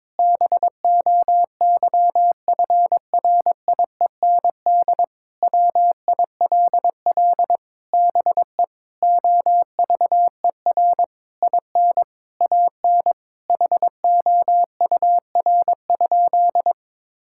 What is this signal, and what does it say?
Morse code, heard as a keyed tone, about 22 wpm, 700 Hz. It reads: BOYFRIEND WILL BE OVER IN AN HOUR?
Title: Sentences from Top 1000 Words - Review Part 5 - 22wpm